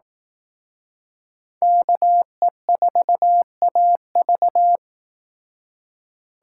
Morse code, heard as a keyed tone, about 18 wpm, 700 Hz. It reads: KE4AV
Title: Speed Racing - USA 2x2 Call Signs 15wpm